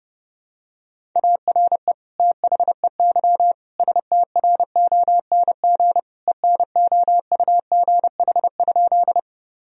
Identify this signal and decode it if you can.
Morse code, heard as a keyed tone, about 30 wpm, 700 Hz. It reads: ARE THEY STRONG ENOUGH?
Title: Sentences from Top 1000 Words - Review Part 3 - 30wpm